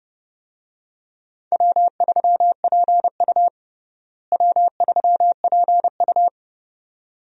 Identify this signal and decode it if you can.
Morse code, heard as a keyed tone, about 30 wpm, 700 Hz. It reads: W3PU W3PU
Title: USA Call Signs - 4 Characters - Repeated 2x - 30wpm